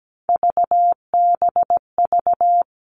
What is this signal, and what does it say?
Morse code, heard as a keyed tone, about 17 wpm, 700 Hz. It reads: VBV
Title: All Cut Numbers - 3-Digit 17wpm